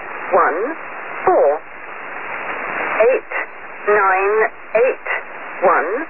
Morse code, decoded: none